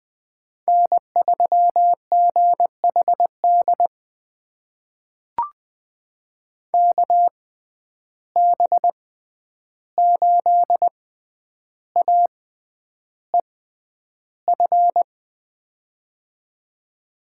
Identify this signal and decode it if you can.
Morse code, heard as a keyed tone, about 20 wpm, spaced as for 10 wpm, 700 Hz. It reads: N3GHD KB8AEF
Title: USA Call Signs - 6x Character Spacing - 20wpm